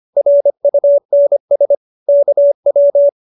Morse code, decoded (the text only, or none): RUNS KW